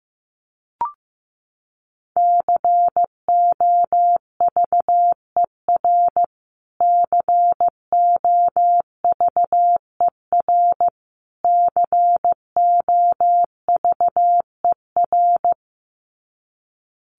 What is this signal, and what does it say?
Morse code, heard as a keyed tone, about 15 wpm, 700 Hz. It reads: COVER COVER COVER
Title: Top 500 Words Repeated 15wpm